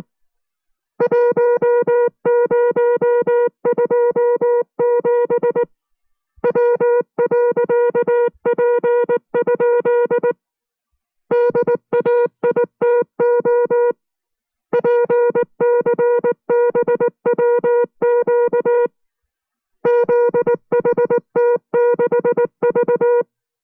Morse code, heard as a keyed tone, about 19 wpm, 480 Hz. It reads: 1027 W.P? DAITO PCBWQ ZHT6V